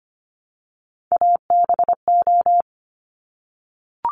Morse code, heard as a keyed tone, about 25 wpm, 700 Hz. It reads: ABO
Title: Top Three Letter Combinations in Words 25wpm